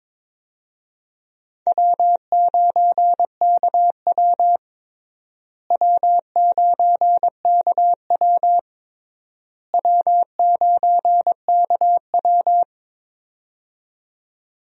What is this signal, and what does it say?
Morse code, heard as a keyed tone, about 22 wpm, 700 Hz. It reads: W9KW W9KW W9KW
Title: USA Call Signs - 4 Characters - Repeated 3x - 22wpm